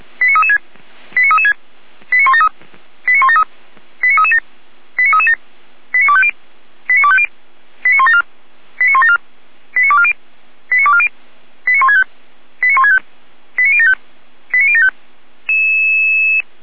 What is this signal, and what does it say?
Five-tone selective call (ZVEI), heard as quick runs of five tones, tones of about 70 ms. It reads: 89336 89336 89173 89173 89337 89337 89250 89250 89174 89174 89240 89240 89165 89165 89075 89075